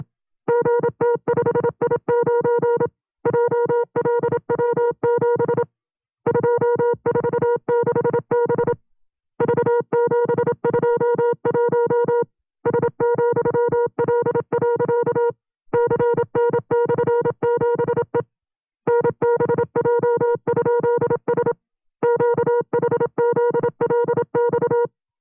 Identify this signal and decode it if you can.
Morse code, heard as a keyed tone, about 27 wpm, 480 Hz. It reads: GT5I9 JLW7 246B V721 S,L. CN/7E NBJ?S QHZLX
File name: FreeFiles / CW-practice / 25WPM-side2